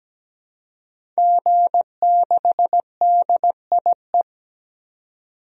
Morse code, heard as a keyed tone, about 17 wpm, 700 Hz. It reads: G6DIE
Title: DXCC-100 Call Signs - 5 Characters - 17wpm